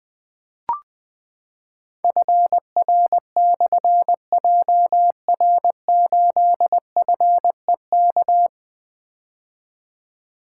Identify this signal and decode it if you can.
Morse code, heard as a keyed tone, about 20 wpm, 700 Hz. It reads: FR/JR8FEK
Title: Difficult Call Signs 20wpm